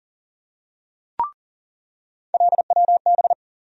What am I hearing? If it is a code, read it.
Morse code, heard as a keyed tone, about 40 wpm, 700 Hz. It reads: LWB
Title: USA Call Sign Postfixes - 3 Letters 40wpm